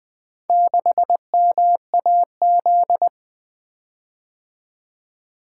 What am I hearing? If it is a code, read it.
Morse code, heard as a keyed tone, about 20 wpm, 700 Hz. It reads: 6MAZ
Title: Four Letter-Number Combinations - 20wpm